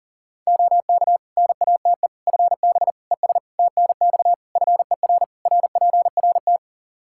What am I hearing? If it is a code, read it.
Morse code, heard as a keyed tone, about 40 wpm, 700 Hz. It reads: OK NATE FB ES TNX FER RPRT
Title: Practice QSO - Exchange 2 - Info - 40wpm